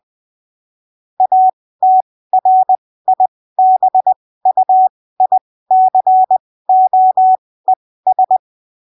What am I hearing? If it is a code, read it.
Morse code, heard as a keyed tone, about 20 wpm, spaced as for 15 wpm, 750 Hz. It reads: ATRIBUICOES
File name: arquivos / texto / texto 12 20ppm